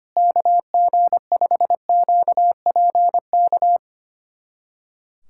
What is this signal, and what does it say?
Morse code, heard as a keyed tone, about 25 wpm, 700 Hz. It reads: KG5QPK
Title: USA Callsign DE Callsign - 6 Characters - 25wpm